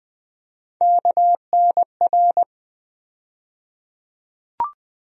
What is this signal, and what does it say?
Morse code, heard as a keyed tone, about 20 wpm, 700 Hz. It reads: KNR
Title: USA Call Sign Postfixes - 3 Letters 20wpm